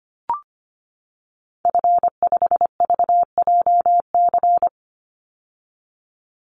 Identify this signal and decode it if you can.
Morse code, heard as a keyed tone, about 25 wpm, 700 Hz. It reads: F5VJC
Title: Call Signs - 25wpm